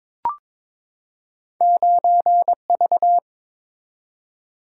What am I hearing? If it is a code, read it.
Morse code, heard as a keyed tone, about 21 wpm, 700 Hz. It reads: 9V
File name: practice / 22wpm / Two Letter-Number Combinations - 22wpm